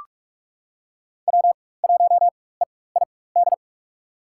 Morse code, heard as a keyed tone, about 46 wpm, spaced as for 21 wpm, 700 Hz. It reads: W1EID
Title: USA Call Signs - 5 Characters - 4x Character Spacing - 45wpm